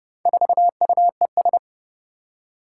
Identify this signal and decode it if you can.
Morse code, heard as a keyed tone, about 30 wpm, 700 Hz. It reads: 4UES
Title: Four Challenging Letter-Number Combinations 30wpm